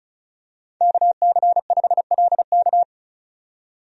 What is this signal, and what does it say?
Morse code, heard as a keyed tone, about 34 wpm, 700 Hz. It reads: KC5LK
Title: USA Call Signs - 5 Characters - 35wpm